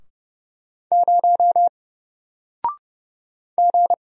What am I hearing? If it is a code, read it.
Morse code, heard as a keyed tone, about 31 wpm, 700 Hz. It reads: 0 G
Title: Single Letter-Number Alt 30wpm